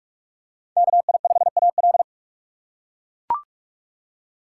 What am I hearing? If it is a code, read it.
Morse code, heard as a keyed tone, about 43 wpm, 700 Hz. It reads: KI5AL
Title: K1USN SST QSO Practice - 45wpm